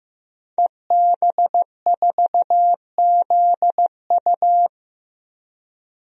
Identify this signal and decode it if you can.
Morse code, heard as a keyed tone, about 15 wpm, 700 Hz. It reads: EB4ZU
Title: DXCC-100 Call Signs - 5 Characters - Repeated 2x - 15wpm